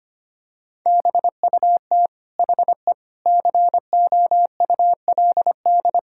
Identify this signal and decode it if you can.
Morse code, heard as a keyed tone, about 25 wpm, 700 Hz. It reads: BUT HE COULD
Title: Sentences from Top 1600 Words 25wpm